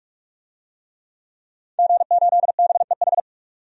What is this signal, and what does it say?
Morse code, heard as a keyed tone, about 46 wpm, 700 Hz. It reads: G8BEH
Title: DXCC-100 Call Signs - 5 Characters - 45wpm